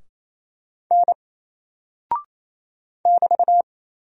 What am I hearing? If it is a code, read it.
Morse code, heard as a keyed tone, about 27 wpm, 700 Hz. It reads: N <BT>
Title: Single Character (Letter:Number:Punctuation:Prosign) 28wpm